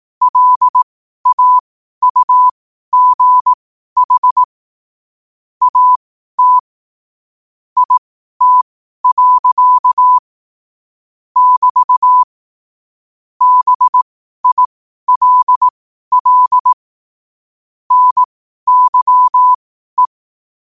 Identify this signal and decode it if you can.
Morse code, heard as a keyed tone, about 18 wpm, spaced as for 12 wpm, 1000 Hz. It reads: LAUGH AT IT. <BT> BILL NYE
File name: qotdfiles / 12wpm